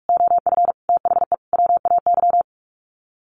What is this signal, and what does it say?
Morse code, heard as a keyed tone, about 44 wpm, 700 Hz. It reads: OF THE WAY
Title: Sets of 4 Words 45wpm